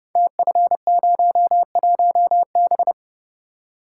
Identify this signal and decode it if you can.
Morse code, heard as a keyed tone, about 30 wpm, 700 Hz. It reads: TF01B
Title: Code Groups - 5 Letter:Numbers - 30wpm